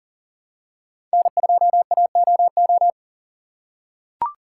Morse code, heard as a keyed tone, about 38 wpm, 700 Hz. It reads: N2AOO
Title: USA Call Signs - 5 Characters - 6x Character Spacing - 40wpm